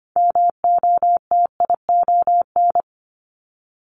Morse code, heard as a keyed tone, about 25 wpm, 700 Hz. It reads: MOTION
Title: Multi-letter Phonemes 25wpm